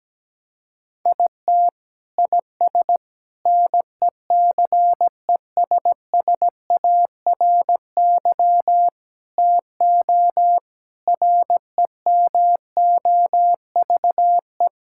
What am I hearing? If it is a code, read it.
Morse code, heard as a keyed tone, about 17 wpm, 700 Hz. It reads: IT IS NECESSARY TO REMOVE
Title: Sentences from Top 1000 Words - Review Part 5 - 17wpm